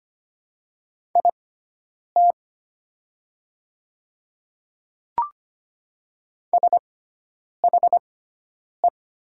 Morse code, heard as a keyed tone, about 25 wpm, spaced as for 9 wpm, 700 Hz. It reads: IT SHE
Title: 6x Character Spacing - Top 100 Words - 25wpm